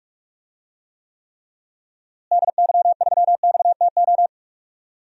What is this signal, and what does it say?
Morse code, heard as a keyed tone, about 42 wpm, 700 Hz. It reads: DY3XTO
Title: DXCC-100 Call Signs - 6 Characters - Repeated 3x - 45wpm